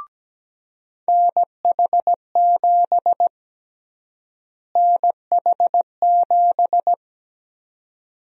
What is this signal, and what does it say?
Morse code, heard as a keyed tone, about 17 wpm, 700 Hz. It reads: NH7 NH7
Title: USA Call Sign Prefixes - Repeated 2x - 17wpm